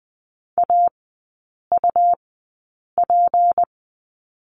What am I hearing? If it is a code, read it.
Morse code, heard as a keyed tone, about 20 wpm, spaced as for 8 wpm, 700 Hz. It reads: AUP
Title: 218 Lesson 35e - Number 0 - Three Characters 20wpm